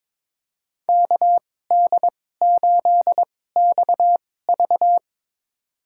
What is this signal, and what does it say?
Morse code, heard as a keyed tone, about 22 wpm, spaced as for 16 wpm, 700 Hz. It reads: KD8XV
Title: USA Call Signs - 2x Character Spacing - 22wpm